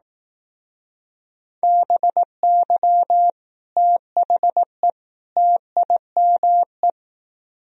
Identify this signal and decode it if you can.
Morse code, heard as a keyed tone, about 18 wpm, 700 Hz. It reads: BY THE TIME
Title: Speed-Racing - Sets of 3 Words 15wpm